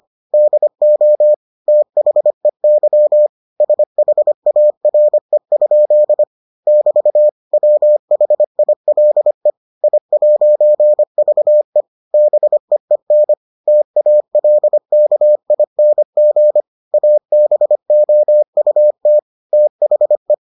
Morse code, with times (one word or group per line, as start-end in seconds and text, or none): 0.3-1.3 DO
1.7-3.3 THEY
3.6-6.2 SHARE?
6.7-7.3 <BT>
7.5-9.5 WHILE
9.8-11.8 I'VE
12.1-13.3 BEEN
13.7-16.6 TALKING
16.9-19.2 ABOUT
19.5-20.3 THE